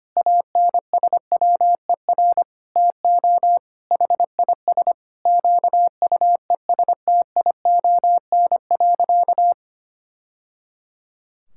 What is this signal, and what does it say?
Morse code, heard as a keyed tone, about 25 wpm, 700 Hz. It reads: ANSWER TO HIS QUESTION.
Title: Sentences from Top 1000 Words - Review Part 6 - 25wpm